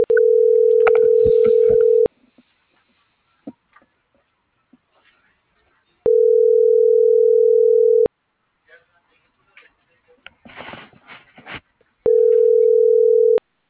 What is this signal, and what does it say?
Ringback tone (North America), 440 + 480 Hz.
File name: CENTRAL MAERSK / out sandovalmsk - .68102